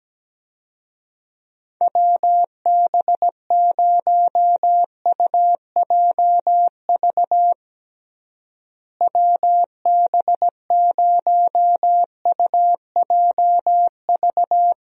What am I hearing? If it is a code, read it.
Morse code, heard as a keyed tone, about 17 wpm, 700 Hz. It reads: WB0UJV WB0UJV
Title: USA Call Signs - 6 Characters - Repeated 2x - 17wpm